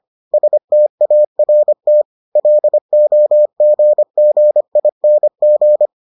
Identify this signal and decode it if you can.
Morse code, heard as a keyed tone, about 25 wpm, 600 Hz. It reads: START LOGGING